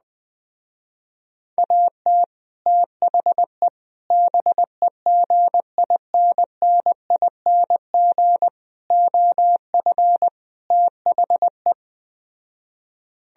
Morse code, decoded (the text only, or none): AT THE BEGINNING OF THE